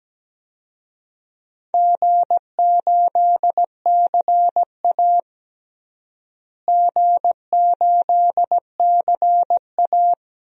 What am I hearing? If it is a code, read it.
Morse code, heard as a keyed tone, about 17 wpm, 700 Hz. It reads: G8CA G8CA
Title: DXCC-100 Call Signs - 4 Characters - Repeated 2x - 17wpm